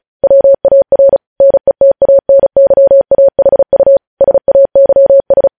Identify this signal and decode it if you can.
Morse code, heard as a keyed tone, about 36 wpm, 550 Hz. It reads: WAR NETANYAHU SAYS
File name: podcast / News headlines-1x-35x35